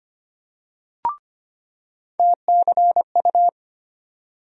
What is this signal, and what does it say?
Morse code, heard as a keyed tone, about 25 wpm, 700 Hz. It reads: TCU
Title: USA Call Sign Postfixes - 3 Letters 25wpm